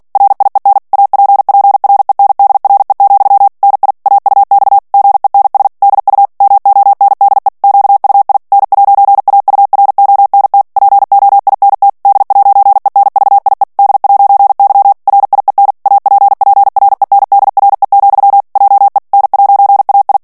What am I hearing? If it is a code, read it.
Morse code, heard as a keyed tone, about 48 wpm, 800 Hz. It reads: RIEN APPRENDRE, NI AUX GENS DU MONDE QUI N'AURONT POINT D'ENVIE D'Y RIEN APPRENDRE, JE N'AI
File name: CW / FICHIERS-MP3 / FRANCAIS TEXTE CLAIR 2 VITESSE 45 MOTS MINUTES